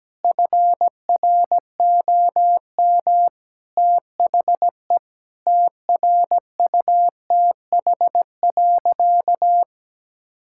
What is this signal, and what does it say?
Morse code, heard as a keyed tone, about 17 wpm, 700 Hz. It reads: FROM THE TRUTH.
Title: Sentences from Top 1600 Words 17wpm